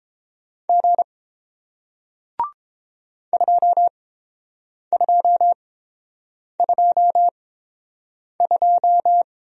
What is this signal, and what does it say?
Morse code, heard as a keyed tone, about 30 wpm, spaced as for 7 wpm, 700 Hz. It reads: G 2222